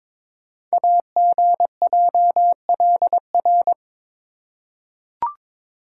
Morse code, heard as a keyed tone, about 22 wpm, 700 Hz. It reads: AGJLR
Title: Code Groups - 5 Letters - 22wpm